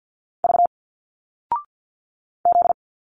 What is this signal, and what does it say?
Morse code, heard as a keyed tone, about 51 wpm, 700 Hz. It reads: V Z
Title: Single Letters 50wpm